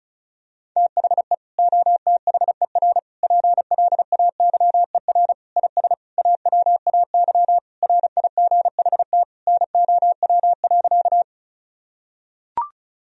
Morse code, heard as a keyed tone, about 36 wpm, 700 Hz. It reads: THE OTHER PLAYER IS AWAY RIGHT NOW.
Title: Sentences from Top 1000 Words - Review Part 4 - 35wpm